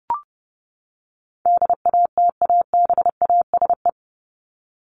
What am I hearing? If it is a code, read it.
Morse code, heard as a keyed tone, about 30 wpm, 700 Hz. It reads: DATABASE
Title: Top 1751-2000 Words 30wpm